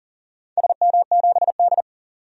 Morse code, heard as a keyed tone, about 40 wpm, 700 Hz. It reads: SM7D
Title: DXCC-100 Call Signs - 4 Characters - 40wpm